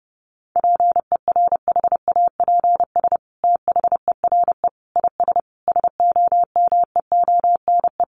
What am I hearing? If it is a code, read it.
Morse code, heard as a keyed tone, about 30 wpm, 700 Hz. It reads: PERHAPS THERE IS SOMEONE
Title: Sentences from Top 600 Words 30wpm